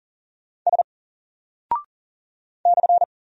Morse code, heard as a keyed tone, about 40 wpm, 700 Hz. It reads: S /
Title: Single Character (Letter:Number:Punctuation:Prosign) 40wpm